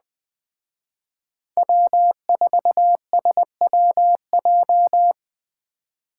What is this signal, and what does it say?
Morse code, heard as a keyed tone, about 20 wpm, 700 Hz. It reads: W4SWJ
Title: Speed Racing - USA 1x3 Call Signs 17wpm